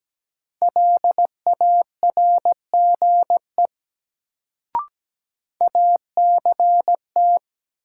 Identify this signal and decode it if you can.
Morse code, heard as a keyed tone, about 17 wpm, 700 Hz. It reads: LARGE ACT
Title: Sentences from Top 200 Words - New Word Drill - 17wpm